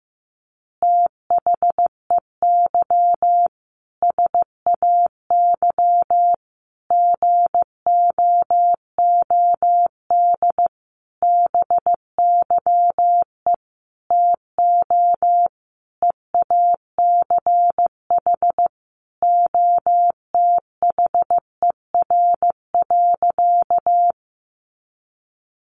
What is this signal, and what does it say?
Morse code, heard as a keyed tone, about 15 wpm, 700 Hz. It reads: THEY SAY GOOD BYE TO EACH OTHER.
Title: Easy Phrases 15wpm